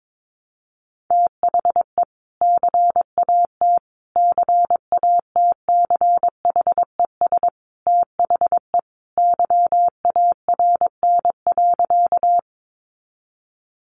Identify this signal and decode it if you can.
Morse code, heard as a keyed tone, about 22 wpm, 700 Hz. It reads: THE CAT CATCHES THE YARN.
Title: Additional Easy Phrases 22wpm